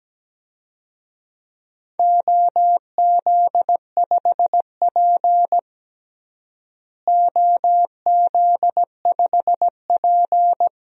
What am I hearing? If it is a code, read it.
Morse code, heard as a keyed tone, about 17 wpm, 700 Hz. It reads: OZ5P OZ5P
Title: DXCC-100 Call Signs - 4 Characters - Repeated 2x - 17wpm